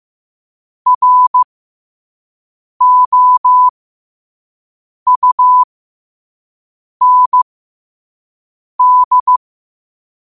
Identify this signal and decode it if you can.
Morse code, heard as a keyed tone, about 15 wpm, spaced as for 5 wpm, 1000 Hz. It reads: ROUND
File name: qotdfiles / 5wpm